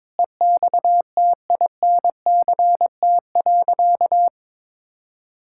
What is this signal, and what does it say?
Morse code, heard as a keyed tone, about 22 wpm, 700 Hz. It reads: EXTINCT.
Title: Pangram Challenge v2 - 22wpm